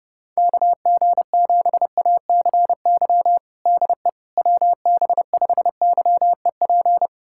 Morse code, heard as a keyed tone, about 32 wpm, 700 Hz. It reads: KG7ACY DE WB5YEP
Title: USA Callsign DE Callsign - 6 Characters - 30wpm